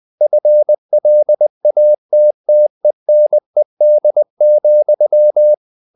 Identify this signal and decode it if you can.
Morse code, heard as a keyed tone, about 20 wpm, 600 Hz. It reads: FLATTENED,
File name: the old man and the sea / the old man and the sea part 1 20wpm